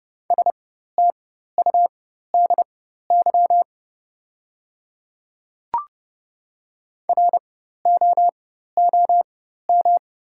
Morse code, heard as a keyed tone, about 30 wpm, spaced as for 14 wpm, 700 Hz. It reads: STUDY ROOM
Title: 4x Character Spacing - Top 300 Words - 30wpm